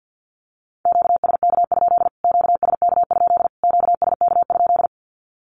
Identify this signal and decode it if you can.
Morse code, heard as a keyed tone, about 50 wpm, 700 Hz. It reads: QSK? QSK? QSK?
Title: Q-Codes Repeated 3x 50wpm